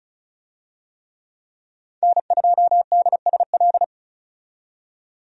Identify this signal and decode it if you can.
Morse code, heard as a keyed tone, about 34 wpm, 700 Hz. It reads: N2DSL
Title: USA Call Signs - 5 Characters - 35wpm